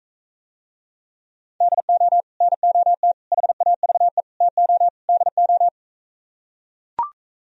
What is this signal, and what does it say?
Morse code, heard as a keyed tone, about 45 wpm, 700 Hz. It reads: DO NOT HAVE TO DO